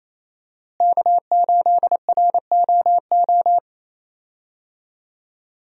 Morse code, heard as a keyed tone, about 28 wpm, 700 Hz. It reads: K8ROO
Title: USA Call Signs - 5 Characters - Repeated 3x - 28wpm